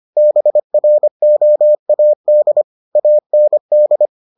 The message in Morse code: BROAD AND

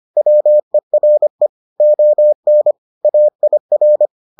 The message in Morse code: WERE ON AIR

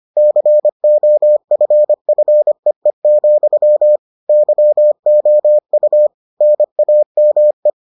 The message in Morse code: COFFEE, YOU NAME

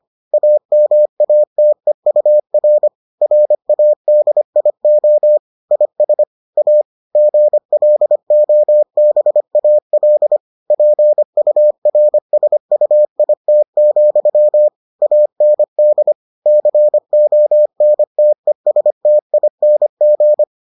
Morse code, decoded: AMATEUR RADIO IS A GLOBAL PURSUIT, AND CONTESTING

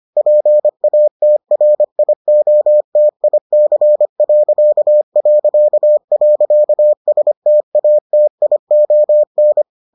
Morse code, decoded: PATRIOTIC...STATION

